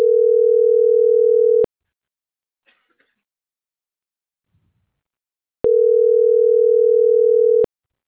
Ringback tone (North America), 440 + 480 Hz.